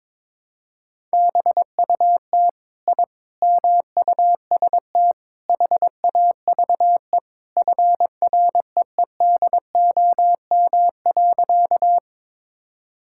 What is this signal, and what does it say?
Morse code, heard as a keyed tone, about 22 wpm, 700 Hz. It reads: BUT I MUST HAVE FREEDOM.
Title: Sentences from Top 1000 Words - Review Part 5 - 22wpm